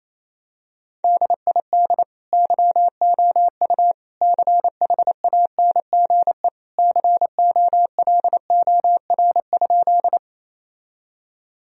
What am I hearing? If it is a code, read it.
Morse code, heard as a keyed tone, about 28 wpm, 700 Hz. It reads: DID YOU CHANGE COLOR?
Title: Sentences from Top 1000 Words - Review Part 2 - 28wpm